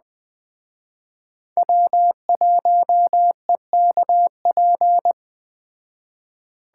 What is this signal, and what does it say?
Morse code, heard as a keyed tone, about 20 wpm, 700 Hz. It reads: W1EKP
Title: Speed Racing - USA 1x3 Call Signs 17wpm